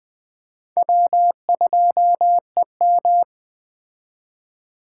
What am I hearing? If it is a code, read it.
Morse code, heard as a keyed tone, about 20 wpm, 700 Hz. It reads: W2EM